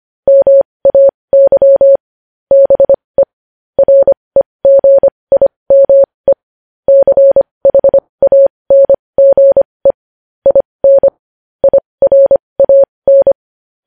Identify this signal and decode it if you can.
Morse code, heard as a keyed tone, about 25 wpm, spaced as for 20 wpm, 550 Hz. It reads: MAY BE REGIME CHANGE IN IRAN
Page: news headlines in morse code at 20 wpm